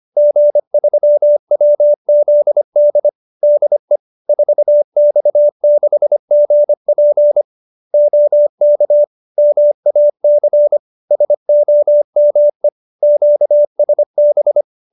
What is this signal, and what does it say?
Morse code, heard as a keyed tone, about 25 wpm, 590 Hz. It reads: G3WZD DE 4X6GP OK MAC SOME QSB
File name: QSO310 25